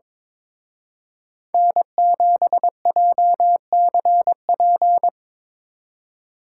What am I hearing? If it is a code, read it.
Morse code, heard as a keyed tone, about 25 wpm, 700 Hz. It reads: N7JCP
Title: Speed Racing - USA 1x3 Call Signs 22wpm